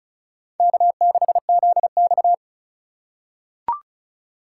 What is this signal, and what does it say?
Morse code, heard as a keyed tone, about 36 wpm, 700 Hz. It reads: K6ZX
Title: K1USN SST QSO Practice - 35wpm